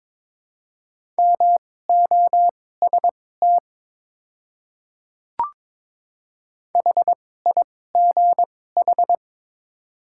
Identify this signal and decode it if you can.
Morse code, heard as a keyed tone, about 22 wpm, spaced as for 16 wpm, 700 Hz. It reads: MOST HIGH
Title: 2x Character Spacing - Top 200 Words - 22wpm